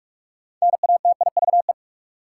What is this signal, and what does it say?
Morse code, heard as a keyed tone, about 42 wpm, 700 Hz. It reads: NATIVE
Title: Six Letter Words - 45wpm